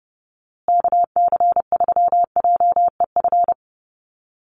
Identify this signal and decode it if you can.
Morse code, heard as a keyed tone, about 30 wpm, 700 Hz. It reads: KC3JEF
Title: USA Call Signs Repeated 3x 30wpm